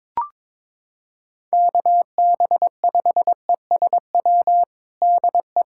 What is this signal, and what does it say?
Morse code, heard as a keyed tone, about 22 wpm, 700 Hz. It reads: KB5ESW DE
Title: USA Callsign DE Callsign 22wpm